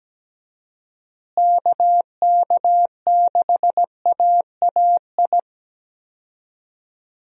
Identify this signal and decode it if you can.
Morse code, heard as a keyed tone, about 17 wpm, 700 Hz. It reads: KK6AAI
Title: USA Call Signs - 6 Characters - Repeated 3x - 17wpm